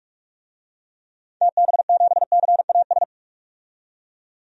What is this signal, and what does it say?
Morse code, heard as a keyed tone, about 41 wpm, 700 Hz. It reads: TB7CAS
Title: DXCC-100 Call Signs - 6 Characters - 45wpm